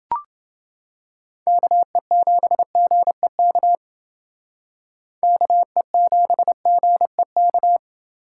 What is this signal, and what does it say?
Morse code, heard as a keyed tone, about 29 wpm, 700 Hz. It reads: KE7GEK KE7GEK